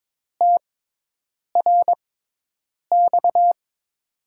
Morse code, heard as a keyed tone, about 21 wpm, spaced as for 8 wpm, 700 Hz. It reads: TRX
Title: USA Call Signs - 5 Characters - 6x Character Spacing - 22wpm